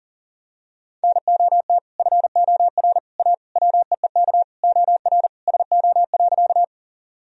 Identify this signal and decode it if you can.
Morse code, heard as a keyed tone, about 40 wpm, 700 Hz. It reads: NOT FOR A WEEK OR SO.
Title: Sentences from Top 1000 Words - Review Part 2 - 40wpm